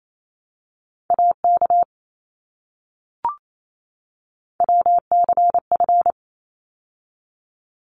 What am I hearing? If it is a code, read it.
Morse code, heard as a keyed tone, about 28 wpm, 700 Hz. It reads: AK WCF
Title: ARRL Sections 28wpm